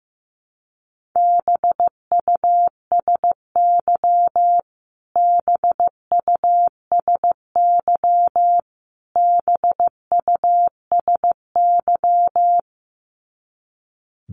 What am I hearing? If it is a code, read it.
Morse code, heard as a keyed tone, about 15 wpm, 700 Hz. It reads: BUSY BUSY BUSY
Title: Top 500 Words Repeated 15wpm